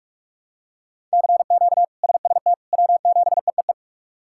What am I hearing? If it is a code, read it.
Morse code, heard as a keyed tone, about 43 wpm, 700 Hz. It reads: CQ SST W7EEE
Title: K1USN SST QSO Practice - 45wpm